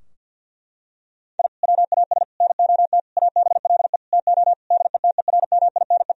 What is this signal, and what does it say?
Morse code, heard as a keyed tone, about 50 wpm, 700 Hz. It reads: I WAS NOT ABLE TO DETERMINE
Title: Sentences from Top 1000 Words - Review Part 4 - 50wpm